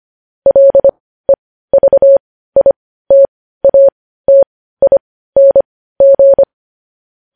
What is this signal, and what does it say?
Morse code, heard as a keyed tone, about 25 wpm, spaced as for 15 wpm, 550 Hz. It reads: LEVITATING
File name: podcast / News headlines-1x-25x15